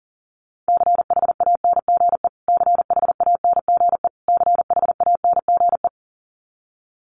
Morse code, heard as a keyed tone, about 40 wpm, 700 Hz. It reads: CHANGE CHANGE CHANGE
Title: Top 200 Words Repeated 40wpm